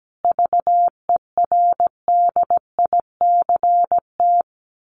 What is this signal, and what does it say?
Morse code, heard as a keyed tone, about 17 wpm, 700 Hz. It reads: VERDICT
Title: A Sound Of Thunder 17wpm